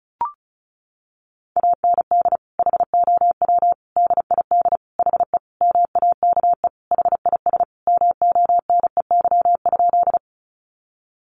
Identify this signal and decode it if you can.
Morse code, heard as a keyed tone, about 34 wpm, 700 Hz. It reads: AND HOW DID HE MAKE HIS MONEY?
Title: Sentences from Top 400 Words 35wpm